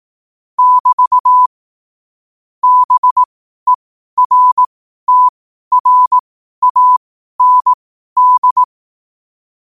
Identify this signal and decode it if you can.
Morse code, heard as a keyed tone, about 18 wpm, spaced as for 12 wpm, 1000 Hz. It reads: <BT> BERTRAND
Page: New quotes every day in morse code at 12 Words per minute.